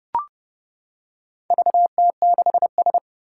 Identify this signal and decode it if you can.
Morse code, heard as a keyed tone, about 28 wpm, 700 Hz. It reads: VT6S
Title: Four Challenging Letter-Number Combinations 30wpm